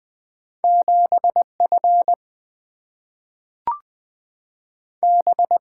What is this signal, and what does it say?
Morse code, heard as a keyed tone, about 20 wpm, 700 Hz. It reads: 7 F B
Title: Two Letter-Number Combinations - 20wpm